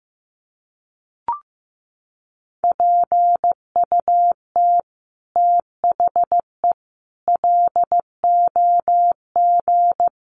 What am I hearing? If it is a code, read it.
Morse code, heard as a keyed tone, about 15 wpm, 700 Hz. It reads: PUT THE LOG
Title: Sentences from Top 1400 Words 15wpm